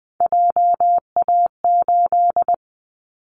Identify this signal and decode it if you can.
Morse code, heard as a keyed tone, about 20 wpm, 700 Hz. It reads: JA8
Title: Callsign Prefixes for Top 50 DXCC - 20wpm